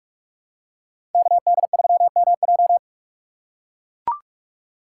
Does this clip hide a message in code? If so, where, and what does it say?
Yes: Morse code from 1.1 to 2.8 s, heard as a keyed tone, about 44 wpm, 700 Hz. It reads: KD3MJ